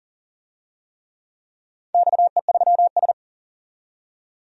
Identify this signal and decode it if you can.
Morse code, heard as a keyed tone, about 40 wpm, 700 Hz. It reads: XE3S
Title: DXCC-100 Call Signs - 4 Characters - 40wpm